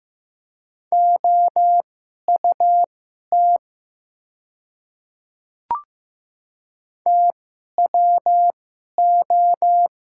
Morse code, heard as a keyed tone, about 15 wpm, spaced as for 11 wpm, 700 Hz. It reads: OUT TWO
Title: 2x Character Spacing - Top 100 Words - 15wpm